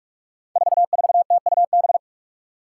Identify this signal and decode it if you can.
Morse code, heard as a keyed tone, about 44 wpm, 700 Hz. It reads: 44TUB
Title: Five Challenging Letter-Number Combinations 45wpm